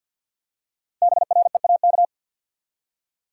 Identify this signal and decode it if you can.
Morse code, heard as a keyed tone, about 47 wpm, 700 Hz. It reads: BREAK
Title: Five Letter Words - 50wpm